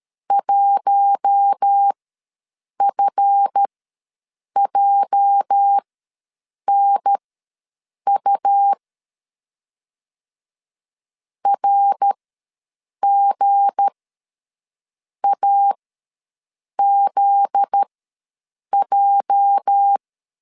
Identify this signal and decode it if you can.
Morse code, heard as a keyed tone, about 13 wpm, spaced as for 7 wpm, 800 Hz. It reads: 1FJNU RGAZJ
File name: LEZIONI / 8 YJ45 / 8 YJ ws 12wpm 3342